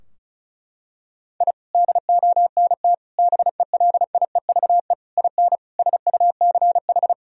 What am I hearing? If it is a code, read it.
Morse code, heard as a keyed tone, about 34 wpm, 700 Hz. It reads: I DONT BELIEVE IN SUCH
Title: Sentences from Top 1000 Words - Review Part 2 - 35wpm